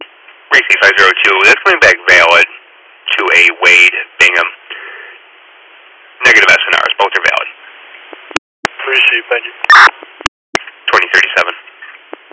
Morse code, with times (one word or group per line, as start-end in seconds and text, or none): none